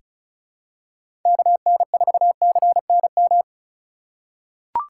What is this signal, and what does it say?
Morse code, heard as a keyed tone, about 36 wpm, 700 Hz. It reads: KN4CNM
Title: USA Call Signs - 2x Character Spacing - 35wpm